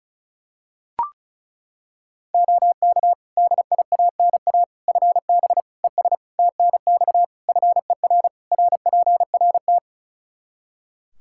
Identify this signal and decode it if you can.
Morse code, heard as a keyed tone, about 36 wpm, 700 Hz. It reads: OK DIANA FB ES TNX FER RPRT
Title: Practice QSO - Exchange 2 - Info - 35wpm